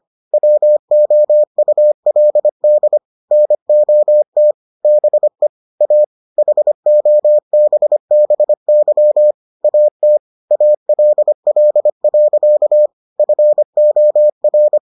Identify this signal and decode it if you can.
Morse code, heard as a keyed tone, about 25 wpm, 600 Hz. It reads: WOULD NOT BE A HOBBY AT ALL. FOR